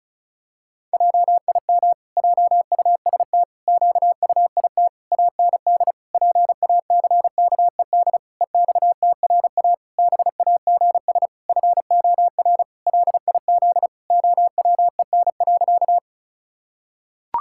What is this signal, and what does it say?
Morse code, heard as a keyed tone, about 34 wpm, 700 Hz. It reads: JIM JUST QUIT AND PACKED EXTRA BAGS FOR LIZ OWEN.
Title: Pangram Challenge v2 - 35wpm